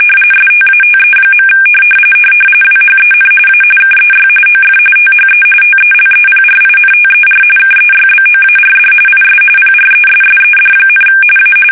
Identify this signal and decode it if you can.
RTTY, about 45.45 baud. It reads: STATE. WHEN A KEY OF THE TELEPRINTER KEYBOARD IS PRESSED, A
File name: fsk850 / teletype-RTTY